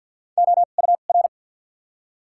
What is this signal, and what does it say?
Morse code, heard as a keyed tone, about 52 wpm, 700 Hz. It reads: O U R
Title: 2x Character Spacing - Top 200 Words - 50wpm